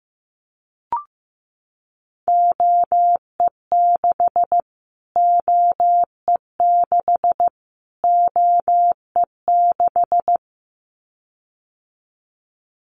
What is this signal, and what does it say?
Morse code, heard as a keyed tone, about 15 wpm, 700 Hz. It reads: OE6 OE6 OE6
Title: Callsign Prefixes for Top 25 DXCC - Repeated 3x - 15wpm